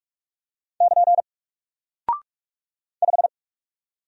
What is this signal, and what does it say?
Morse code, heard as a keyed tone, about 44 wpm, 700 Hz. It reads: <KN> 5
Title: Single Character (Letter:Number:Punctuation:Prosign) 45wpm